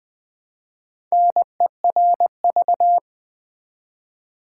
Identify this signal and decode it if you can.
Morse code, heard as a keyed tone, about 20 wpm, 700 Hz. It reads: NERV
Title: Build A Word - 1 Letter At A Time - 5-Letter Words - 20wpm